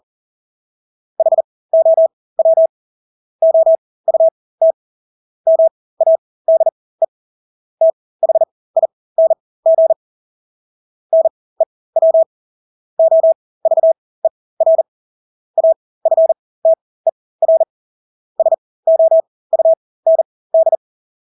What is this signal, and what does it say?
Morse code, heard as a keyed tone, about 40 wpm, spaced as for 20 wpm, 650 Hz. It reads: HOW OUT MADE THING NEW OVER AFTER SOUND
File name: Engelske ord 40-20 wpm